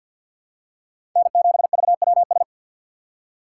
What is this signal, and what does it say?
Morse code, heard as a keyed tone, about 53 wpm, 700 Hz. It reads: N7VWS
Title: USA Call Signs - 5 Characters - 4x Character Spacing - 50wpm